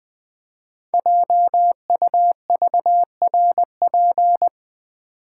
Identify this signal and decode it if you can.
Morse code, heard as a keyed tone, about 20 wpm, 700 Hz. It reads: JUVRP
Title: Code Groups - 5 Letters - 20wpm